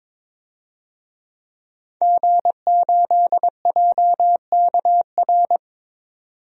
Morse code, heard as a keyed tone, about 22 wpm, 700 Hz. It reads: G8JKR
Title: DXCC-100 Call Signs - 5 Characters - 22wpm